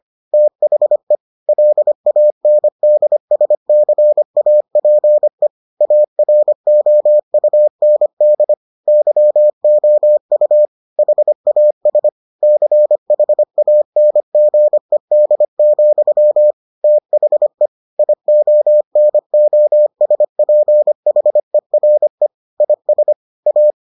Morse code, heard as a keyed tone, about 25 wpm, 600 Hz. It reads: THE LANDSCAPE AROUND YOU HAS CHANGED, THE IONOSPHERE IS A